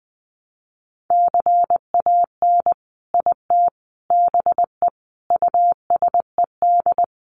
Morse code, heard as a keyed tone, about 20 wpm, 700 Hz. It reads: CAN IT BE USED
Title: Sentences from Top 600 Words 20wpm